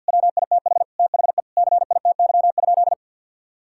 Morse code, heard as a keyed tone, about 47 wpm, 700 Hz. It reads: WITH THE CITY?
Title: Sentences from Top 1600 Words 50wpm